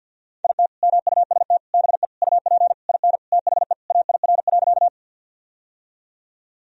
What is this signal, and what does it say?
Morse code, heard as a keyed tone, about 47 wpm, 700 Hz. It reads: IT MUST BE UP IN THE AIR.
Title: Sentences from Top 1000 Words - Review Part 1 - 50wpm